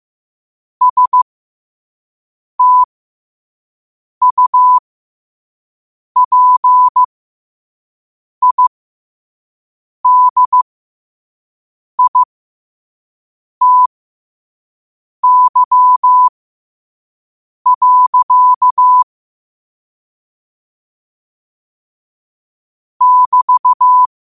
Morse code, heard as a keyed tone, about 15 wpm, spaced as for 5 wpm, 1000 Hz. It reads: STUPIDITY. <BT>
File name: qotdfiles / 5wpm